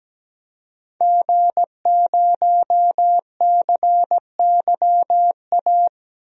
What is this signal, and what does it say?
Morse code, heard as a keyed tone, about 17 wpm, 700 Hz. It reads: G0CYA
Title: DXCC-100 Call Signs - 5 Characters - 17wpm